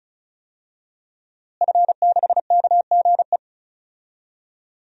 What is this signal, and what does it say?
Morse code, heard as a keyed tone, about 36 wpm, 700 Hz. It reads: F6KGE
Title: DXCC-100 Call Signs - 5 Characters - Repeated 2x - 35wpm